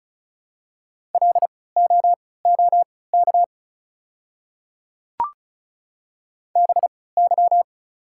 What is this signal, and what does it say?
Morse code, heard as a keyed tone, about 36 wpm, spaced as for 20 wpm, 700 Hz. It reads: LOOK BY
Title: 3x Character Spacing - Top 300 Words - 35wpm